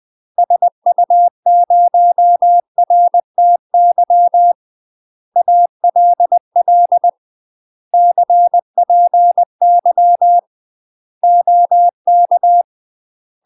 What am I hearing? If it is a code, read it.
Morse code, heard as a keyed tone, about 20 wpm, 700 Hz. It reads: SU0RTY ALL CPY OK
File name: UK MASTER QSOs TRAINING / 20WPM / UK MASTER QSOs 0066